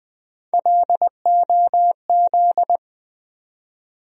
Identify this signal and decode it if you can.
Morse code, heard as a keyed tone, about 20 wpm, 700 Hz. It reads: LOZ
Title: USA Call Sign Postfixes - 3 Letters 20wpm